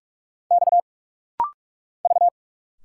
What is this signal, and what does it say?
Morse code, heard as a keyed tone, about 46 wpm, 690 Hz. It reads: X V
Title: Single Letters - Rapid-Fire 45wpm